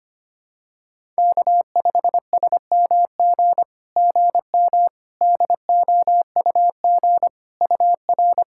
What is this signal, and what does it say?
Morse code, heard as a keyed tone, about 25 wpm, 690 Hz. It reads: K5SMG GM DOUG UR
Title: SOTA QSO Practice 25wpm